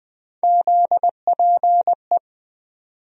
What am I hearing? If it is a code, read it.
Morse code, heard as a keyed tone, about 20 wpm, 700 Hz. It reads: ZPE
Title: USA Call Sign Postfixes - 3 Letters 20wpm